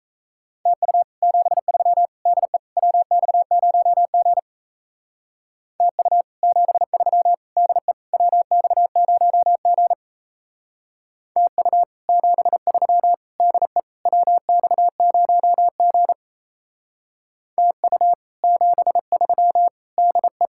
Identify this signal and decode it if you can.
Morse code, heard as a keyed tone, about 32 wpm, 700 Hz. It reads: TU 73 DE WX0G TU 73 DE WX0G TU 73 DE WX0G TU 73 DE
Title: Speed-Racing Common QSO Phrases - 28wpm